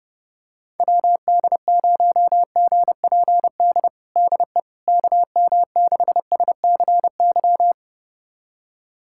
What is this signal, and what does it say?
Morse code, heard as a keyed tone, about 30 wpm, 700 Hz. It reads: WD0GPD DE KM6SCY
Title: USA Callsign DE Callsign 30wpm